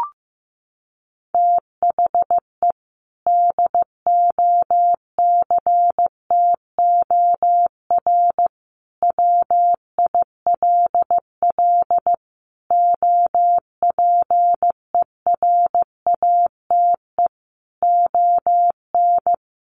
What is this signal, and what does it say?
Morse code, heard as a keyed tone, about 15 wpm, 700 Hz. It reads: THE DOCTOR WILL OPERATE ON
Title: Sentences from Top 1500 Words 15wpm